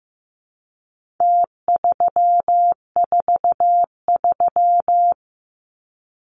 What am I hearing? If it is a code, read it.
Morse code, heard as a keyed tone, about 15 wpm, 700 Hz. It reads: T343
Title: Numbers - Contest Style - 4-Digit 15wpm